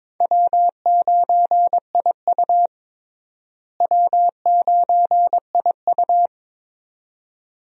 Morse code, heard as a keyed tone, about 21 wpm, 700 Hz. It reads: W9IU W9IU
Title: USA Call Signs - 4 Characters - Repeated 2x - 22wpm